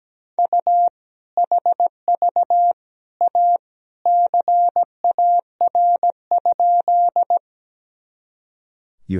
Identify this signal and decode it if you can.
Morse code, heard as a keyed tone, about 17 wpm, 700 Hz. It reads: U HV A CAR?
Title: Sentences from Top 300 Words and Abbreviations 17wpm